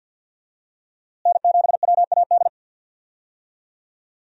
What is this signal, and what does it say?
Morse code, heard as a keyed tone, about 52 wpm, 700 Hz. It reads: N7WAD
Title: USA Call Signs - 5 Characters - Repeated 2x - 50wpm